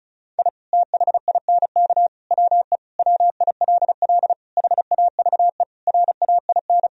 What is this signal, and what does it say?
Morse code, heard as a keyed tone, about 36 wpm, 700 Hz. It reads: I THINK WE WILL HAVE RAIN
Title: Sentences from Top 1000 Words - Review Part 1 - 35wpm